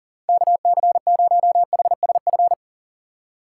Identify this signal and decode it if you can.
Morse code, heard as a keyed tone, about 51 wpm, 700 Hz. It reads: KC0HSF